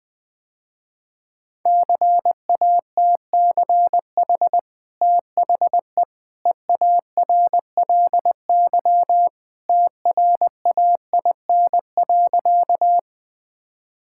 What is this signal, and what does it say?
Morse code, heard as a keyed tone, about 20 wpm, 700 Hz. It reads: CATCH THE EARLY TRAIN.
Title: Easy Sentences with Hints - 20wpm